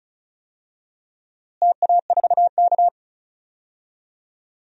Morse code, heard as a keyed tone, about 34 wpm, 700 Hz. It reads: TA4K
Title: DXCC-100 Call Signs - 4 Characters - Repeated 2x - 35wpm